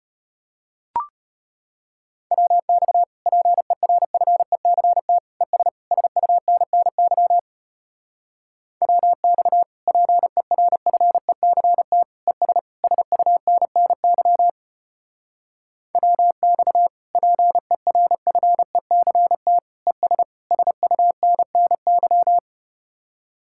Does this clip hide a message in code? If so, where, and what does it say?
Yes: Morse code from 2.3 to 22.4 s, heard as a keyed tone, about 33 wpm, 700 Hz. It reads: WX PERFECT ES SUNNY WX PERFECT ES SUNNY WX PERFECT ES SUNNY